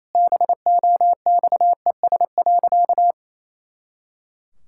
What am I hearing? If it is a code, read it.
Morse code, heard as a keyed tone, about 28 wpm, 700 Hz. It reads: BOXES.
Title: Pangram Challenge v2 - 28wpm